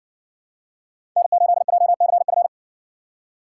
Morse code, heard as a keyed tone, about 56 wpm, 700 Hz. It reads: N8JCF